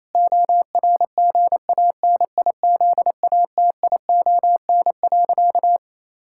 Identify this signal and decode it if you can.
Morse code, heard as a keyed tone, about 28 wpm, 700 Hz. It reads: ORGANIZATION.
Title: Sentences from Top 1000 Words - Review Part 5 - 28wpm